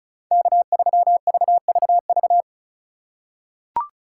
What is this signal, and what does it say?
Morse code, heard as a keyed tone, about 36 wpm, 700 Hz. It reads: K3VVV
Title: USA Call Signs - 2x Character Spacing - 35wpm